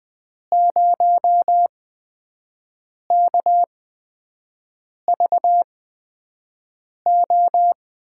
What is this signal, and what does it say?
Morse code, heard as a keyed tone, about 20 wpm, spaced as for 5 wpm, 700 Hz. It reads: 0KVO
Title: USA Call Signs - 8x Character Spacing - 20wpm